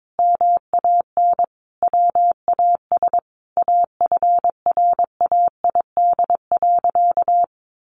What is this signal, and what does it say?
Morse code, heard as a keyed tone, about 22 wpm, 700 Hz. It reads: MAN WAS AFRAID.
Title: Sentences from Top 1500 Words 22wpm